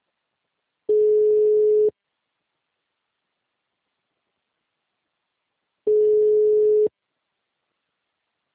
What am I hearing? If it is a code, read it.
Ringback tone (Europe, 425 Hz).